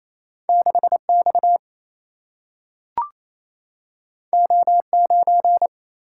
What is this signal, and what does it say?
Morse code, heard as a keyed tone, about 28 wpm, 700 Hz. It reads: 6 X O 9
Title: Single-Letter Single-Number 28wpm